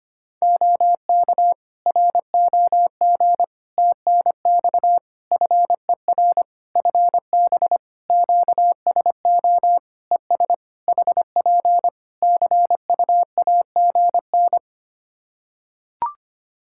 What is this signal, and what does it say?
Morse code, heard as a keyed tone, about 25 wpm, 700 Hz. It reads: OK ROG TNX FER FB QSO ES HP CUAGN
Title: Practice QSO - Exchange 4 - Ending - 25wpm